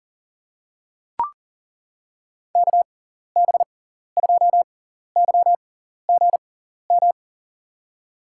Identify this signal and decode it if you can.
Morse code, heard as a keyed tone, about 38 wpm, spaced as for 14 wpm, 700 Hz. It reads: KB2YGM
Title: USA Call Signs - 6 Characters - 6x Character Spacing - 40wpm